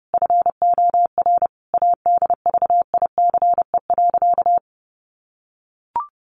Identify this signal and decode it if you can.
Morse code, heard as a keyed tone, about 30 wpm, 700 Hz. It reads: FOR ADVICE.
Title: Sentences from Top 1100 Words 30wpm